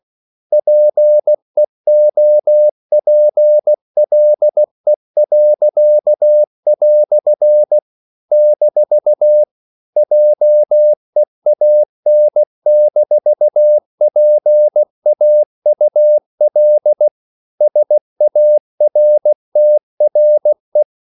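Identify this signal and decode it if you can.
Morse code, heard as a keyed tone, about 16 wpm, 600 Hz. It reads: PEOPLE." - JEAN-PAUL SARTRE
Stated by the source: English quotations sent in CW as practice at 16 words per minute.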